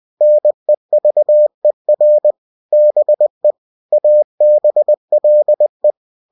Morse code, heard as a keyed tone, about 20 wpm, 600 Hz. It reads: NEVER BE ABLE
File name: cw-books / winnie-the-pooh / winnie-20wpm-page-0017